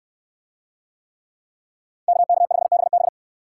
Morse code, heard as a keyed tone, about 61 wpm, 700 Hz. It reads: KK5BB